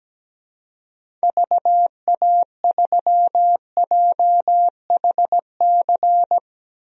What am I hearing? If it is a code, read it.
Morse code, heard as a keyed tone, about 17 wpm, 700 Hz. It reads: VA3JHC
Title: DXCC-100 Call Signs - 6 Characters - 17wpm